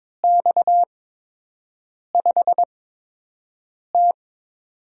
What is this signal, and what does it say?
Morse code, heard as a keyed tone, about 22 wpm, spaced as for 6 wpm, 700 Hz. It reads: X5T
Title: USA Call Signs - 8x Character Spacing - 22wpm